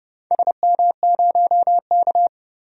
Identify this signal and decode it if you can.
Morse code, heard as a keyed tone, about 30 wpm, 700 Hz. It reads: SM0K
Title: DXCC-100 Call Signs - 4 Characters - 30wpm